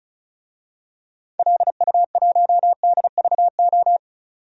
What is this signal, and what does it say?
Morse code, heard as a keyed tone, about 34 wpm, 700 Hz. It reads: LU1DVO
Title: DXCC-100 Call Signs - 6 Characters - 35wpm